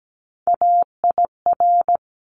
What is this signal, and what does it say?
Morse code, heard as a keyed tone, about 17 wpm, 700 Hz. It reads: AIR
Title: Top 200 Words 17wpm